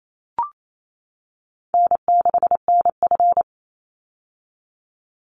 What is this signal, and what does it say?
Morse code, heard as a keyed tone, about 28 wpm, 700 Hz. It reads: N6NF
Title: Easy Call Signs - 28wpm